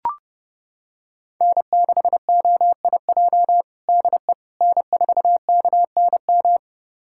Morse code, heard as a keyed tone, about 30 wpm, 700 Hz. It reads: N6OIJ DE N4KNM
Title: USA Callsign DE Callsign - 5 Characters - 30wpm